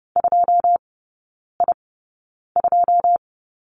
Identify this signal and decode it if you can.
Morse code, heard as a keyed tone, about 30 wpm, spaced as for 9 wpm, 700 Hz. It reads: 2I2
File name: 081 Lesson 14e - Number 2 - Three Characters 30wpm